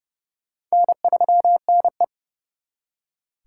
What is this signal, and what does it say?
Morse code, heard as a keyed tone, about 40 wpm, 700 Hz. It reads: N3NE